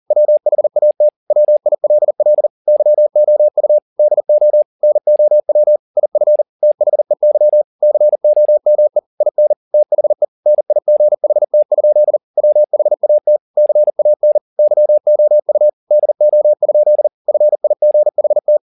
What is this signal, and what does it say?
Morse code, heard as a keyed tone, about 40 wpm, 600 Hz. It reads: WHAT WILL YOU DO NOW IF THEY COME IN THE NIGHT? WHAT CAN YOU DO? FIGHT
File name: the old man and the sea 2 40wpm